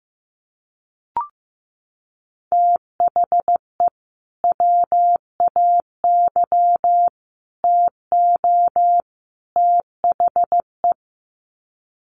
Morse code, heard as a keyed tone, about 15 wpm, 700 Hz. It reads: THE WAY TO THE
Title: Sets of 4 Words 15wpm